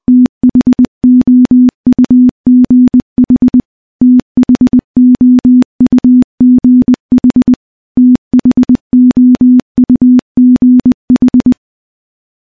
Morse code, heard as a keyed tone, about 20 wpm, 250 Hz. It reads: THOUGH THOUGH THOUGH
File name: morse / morse-training-20WPM